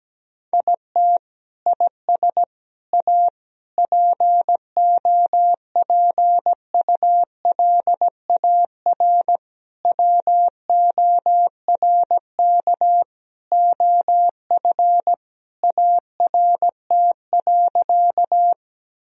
Morse code, heard as a keyed tone, about 17 wpm, 700 Hz. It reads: IT IS A POPULAR WORK OF ART.
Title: Sentences from Top 1000 Words - Review Part 3 - 17wpm